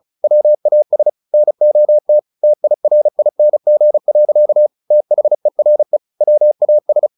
Morse code, heard as a keyed tone, about 34 wpm, 600 Hz. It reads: WAS NOT TIRING. THERE WAS
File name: the old man and the sea / the old man and the sea 1 35wpm